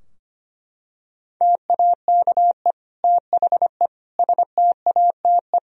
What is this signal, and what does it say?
Morse code, heard as a keyed tone, about 25 wpm, 700 Hz. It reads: TAKE THE STATE
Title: Sentences from Top 1000 Words - Review Part 3 - 25wpm